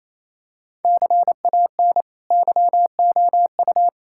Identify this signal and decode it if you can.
Morse code, heard as a keyed tone, about 28 wpm, 700 Hz. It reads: CAN YOU
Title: Sentences from Top 1000 Words - Review Part 5 - 28wpm